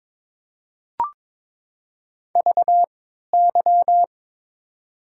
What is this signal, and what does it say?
Morse code, heard as a keyed tone, about 22 wpm, spaced as for 12 wpm, 700 Hz. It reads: VY
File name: Top 90 QSO Elements - 3x Character Spacing - 22wpm